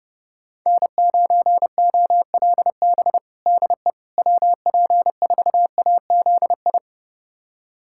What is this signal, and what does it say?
Morse code, heard as a keyed tone, about 32 wpm, 700 Hz. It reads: N9OLB DE WP4AZI
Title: USA Callsign DE Callsign 30wpm